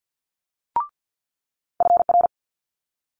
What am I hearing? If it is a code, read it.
Morse code, heard as a keyed tone, about 50 wpm, 700 Hz. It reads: FR
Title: Multi-letter Phonemes Quick Fire 50wpm